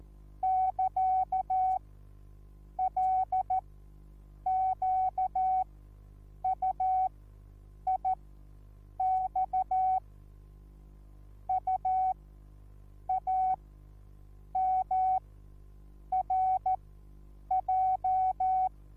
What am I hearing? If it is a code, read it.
Morse code, heard as a keyed tone, about 14 wpm, spaced as for 7 wpm, 750 Hz. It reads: <KA> LQUIX UAMRJ
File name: Onlinelehrgang / morsen / DJ4UF Morsen21